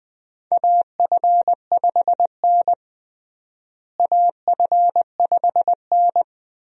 Morse code, heard as a keyed tone, about 20 wpm, 700 Hz. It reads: AF5N AF5N
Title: USA Call Signs - 4 Characters - Repeated 2x - 20wpm